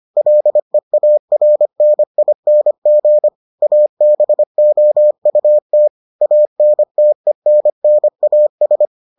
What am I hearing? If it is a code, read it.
Morse code, heard as a keyed tone, about 25 wpm, 600 Hz. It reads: LEARNING ABOUT ANTENNAS